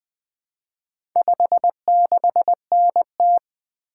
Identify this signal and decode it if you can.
Morse code, heard as a keyed tone, about 20 wpm, 700 Hz. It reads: 56NT
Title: Four Letter-Number Combinations - 20wpm